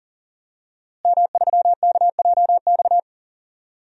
Morse code, heard as a keyed tone, about 38 wpm, 700 Hz. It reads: M3KJX
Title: DXCC-100 Call Signs - 5 Characters - Repeated 3x - 40wpm